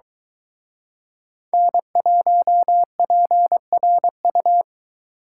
Morse code, heard as a keyed tone, about 23 wpm, 700 Hz. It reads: N1PRU